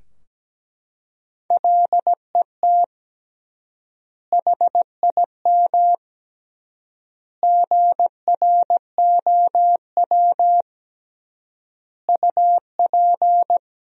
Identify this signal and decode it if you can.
Morse code, heard as a keyed tone, about 17 wpm, 700 Hz. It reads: LET HIM GROW UP
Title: 3x Word-Spacing - Sentences from Top 300 Words - 17wpm